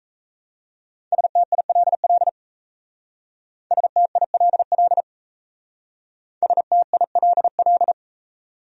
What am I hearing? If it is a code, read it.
Morse code, heard as a keyed tone, about 36 wpm, 690 Hz. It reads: STILL STILL STILL